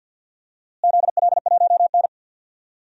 Morse code, heard as a keyed tone, about 47 wpm, 700 Hz. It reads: ZL1N
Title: DXCC-100 Call Signs - 4 Characters - Repeated 3x - 50wpm